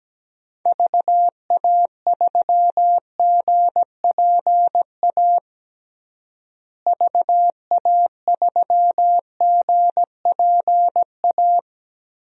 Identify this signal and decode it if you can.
Morse code, heard as a keyed tone, about 17 wpm, 700 Hz. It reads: VA3GPA VA3GPA
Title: DXCC-100 Call Signs - 6 Characters - Repeated 2x - 17wpm